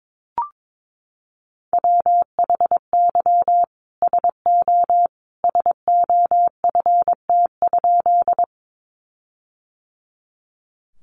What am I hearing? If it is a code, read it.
Morse code, heard as a keyed tone, about 22 wpm, 700 Hz. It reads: WHY SO SOFT?
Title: Sentences from Top 1500 Words 22wpm